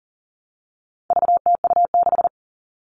Morse code, heard as a keyed tone, about 40 wpm, 700 Hz. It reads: VTU6
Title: All Cut Numbers - 4-Digit 40wpm